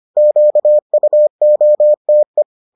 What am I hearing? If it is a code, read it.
Morse code, heard as a keyed tone, about 25 wpm, 600 Hz. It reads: QUOTE